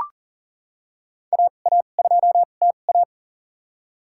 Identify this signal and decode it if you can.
Morse code, heard as a keyed tone, about 38 wpm, spaced as for 28 wpm, 700 Hz. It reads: AA2TA
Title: USA Call Signs - 5 Characters - 2x Character Spacing - 40wpm